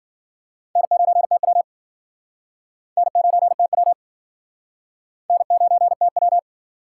Morse code, heard as a keyed tone, about 52 wpm, 700 Hz. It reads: N9TW N9TW N9TW